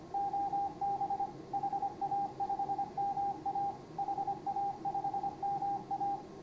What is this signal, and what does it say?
Morse code, heard as a keyed tone, about 25 wpm, 800 Hz, with radio noise. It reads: OB HA5MA HA5MA
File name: CW / HA5MA RHOB HANG CW 2017